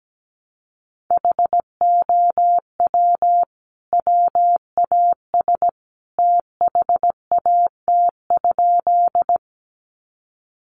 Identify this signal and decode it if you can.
Morse code, heard as a keyed tone, about 17 wpm, 700 Hz. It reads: HOW WAS THAT?
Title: Sentences from Top 100 Words 17wpm